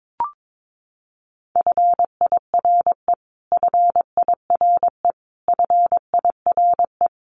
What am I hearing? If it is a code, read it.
Morse code, heard as a keyed tone, about 22 wpm, 700 Hz. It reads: FIRE FIRE FIRE
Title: Top 500 Words Repeated 22wpm